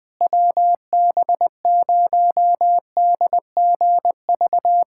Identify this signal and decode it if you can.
Morse code, heard as a keyed tone, about 20 wpm, 700 Hz. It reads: WB0DGV
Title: USA Callsign DE Callsign 20wpm